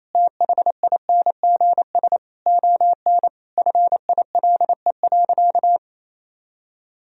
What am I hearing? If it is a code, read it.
Morse code, heard as a keyed tone, about 28 wpm, 700 Hz. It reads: THINGS ON FILE.
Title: Sentences from Top 1000 Words - Review Part 4 - 28wpm